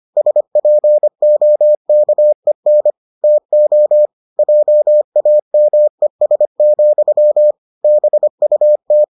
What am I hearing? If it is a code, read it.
Morse code, heard as a keyed tone, about 25 wpm, 600 Hz. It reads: SPOKEN TO JAMES, BUT